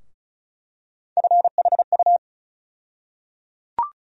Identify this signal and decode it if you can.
Morse code, heard as a keyed tone, about 35 wpm, 700 Hz. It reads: FHU
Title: USA Call Sign Postfixes - 3 Letters 35wpm